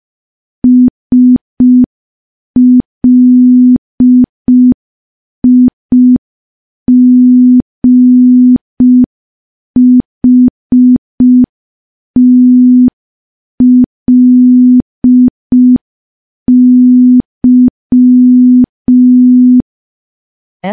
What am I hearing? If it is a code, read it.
Morse code, heard as a keyed tone, about 5 wpm, 250 Hz. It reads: SLIGHTLY
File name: morse / morse-training-05WPM